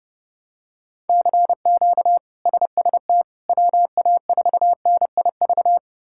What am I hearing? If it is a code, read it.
Morse code, heard as a keyed tone, about 30 wpm, 700 Hz. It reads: CQ SST WA4NIV
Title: K1USN SST QSO Practice - 30wpm